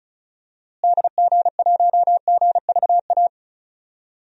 Morse code, heard as a keyed tone, about 36 wpm, 700 Hz. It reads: DG1GVA
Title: DXCC-100 Call Signs - 6 Characters - Repeated 2x - 35wpm